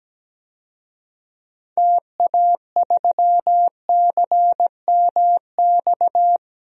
Morse code, heard as a keyed tone, about 17 wpm, 700 Hz. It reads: TA3CMX
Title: DXCC-100 Call Signs - 6 Characters - 17wpm